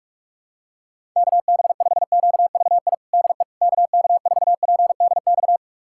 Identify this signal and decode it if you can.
Morse code, heard as a keyed tone, about 46 wpm, 700 Hz. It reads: KB5QVI DE KK4PDX
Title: USA Callsign DE Callsign - 6 Characters - 45wpm